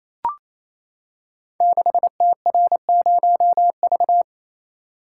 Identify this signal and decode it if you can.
Morse code, heard as a keyed tone, about 27 wpm, 700 Hz. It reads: 6TR0V
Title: Code Groups - 5 Letter:Numbers - 28wpm